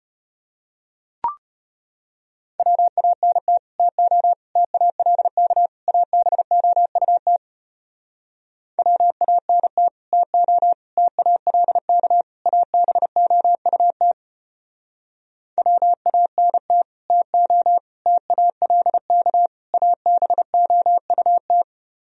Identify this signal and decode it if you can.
Morse code, heard as a keyed tone, about 33 wpm, 700 Hz. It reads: WANT TO TALK ABOUT WANT TO TALK ABOUT WANT TO TALK ABOUT